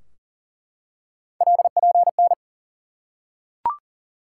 Morse code, heard as a keyed tone, about 40 wpm, 700 Hz. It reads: LPN
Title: K1USN SST QSO Practice 40wpm